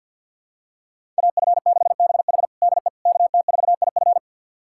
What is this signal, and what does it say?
Morse code, heard as a keyed tone, about 52 wpm, 700 Hz. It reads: AF6BH DE KT4IF
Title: USA Callsign DE Callsign - 5 Characters - 50wpm